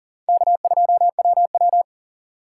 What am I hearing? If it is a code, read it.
Morse code, heard as a keyed tone, about 40 wpm, 700 Hz. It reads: K2WW
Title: USA Call Signs - 4x Character Spacing - 40wpm